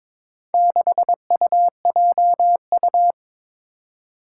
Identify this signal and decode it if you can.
Morse code, heard as a keyed tone, about 22 wpm, 700 Hz. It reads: 6UJU
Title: Four Challenging Letter-Number Combinations 22wpm